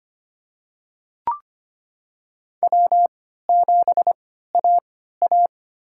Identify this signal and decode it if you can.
Morse code, heard as a keyed tone, about 25 wpm, spaced as for 14 wpm, 700 Hz. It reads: W7AA
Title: USA Call Signs - 3x Character Spacing - 25wpm